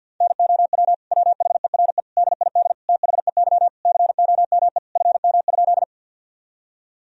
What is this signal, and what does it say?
Morse code, heard as a keyed tone, about 52 wpm, 700 Hz. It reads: NOW WHERE DID THEY COME FM?
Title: Sentences from Top 200 Words and Abbreviations 50wpm